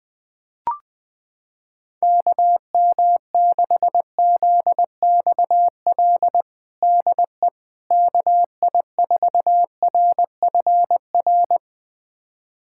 Morse code, heard as a keyed tone, about 21 wpm, 700 Hz. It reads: KM6ZXL DE KI4RFR
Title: USA Callsign DE Callsign 20wpm